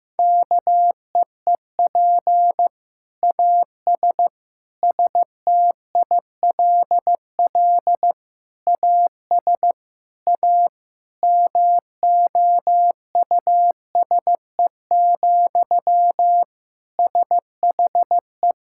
Morse code, heard as a keyed tone, about 15 wpm, 700 Hz. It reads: KEEP AS STILL AS A MOUSE, SHE
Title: Sentences from Top 1000 Words - Review Part 5 - 15wpm